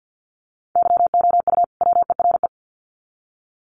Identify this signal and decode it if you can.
Morse code, heard as a keyed tone, about 50 wpm, 700 Hz. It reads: YOU WERE
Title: Sets of 2 Words 50wpm